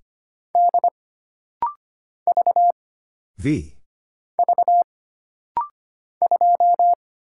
Morse code, heard as a keyed tone, about 25 wpm, 700 Hz. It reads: D V V 2
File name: Single Character (Letter:Number:Punctuation:Prosign) - Rapid-Fire - 25wpm